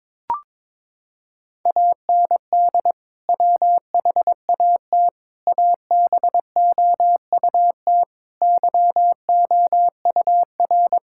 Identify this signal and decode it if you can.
Morse code, heard as a keyed tone, about 22 wpm, 700 Hz. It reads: AND WHAT ABOUT YOUR
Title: Sentences from Top 1000 Words - Review Part 6 - 22wpm